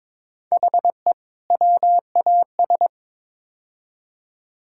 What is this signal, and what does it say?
Morse code, heard as a keyed tone, about 22 wpm, 700 Hz. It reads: HE WAS
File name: practice / 22wpm / Build A Phrase - 5-Word Phrases - 22wpm